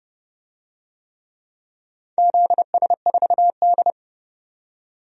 Morse code, heard as a keyed tone, about 30 wpm, 700 Hz. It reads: ZS4D
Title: DXCC-100 Call Signs - 4 Characters - 30wpm